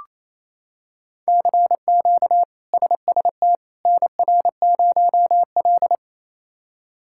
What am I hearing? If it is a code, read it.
Morse code, heard as a keyed tone, about 28 wpm, 700 Hz. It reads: CQ SST NR0L
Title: K1USN SST QSO Practice - 28wpm